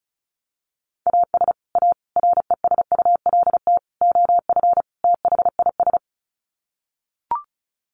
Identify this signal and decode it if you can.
Morse code, heard as a keyed tone, about 34 wpm, 700 Hz. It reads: AS A RESULT OF THIS
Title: Sets of 5 Words 35wpm